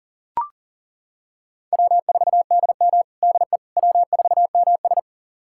Morse code, heard as a keyed tone, about 40 wpm, 700 Hz. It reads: W4DM DE W4MS
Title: USA Callsign DE Callsign - 4 Characters - 40wpm